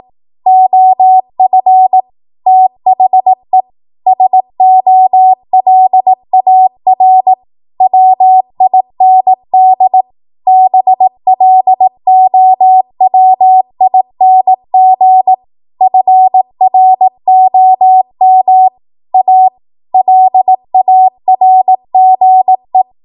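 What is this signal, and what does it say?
Morse code, heard as a keyed tone, about 18 wpm, 750 Hz. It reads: OF THE SOLAR WIND BLOWING FROM A LARGE
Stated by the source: ARRL 18 wpm Morse code practice archive transmission.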